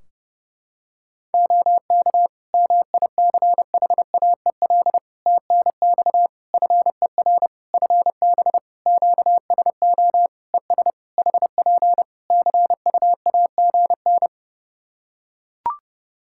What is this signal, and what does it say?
Morse code, heard as a keyed tone, about 32 wpm, 700 Hz. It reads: OK MICHAEL TNX FER FB QSO ES HP CUAGN
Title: K1USN SST QSO Practice 30wpm